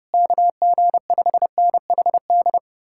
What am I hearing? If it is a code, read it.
Morse code, heard as a keyed tone, about 33 wpm, 700 Hz. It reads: KG5NHD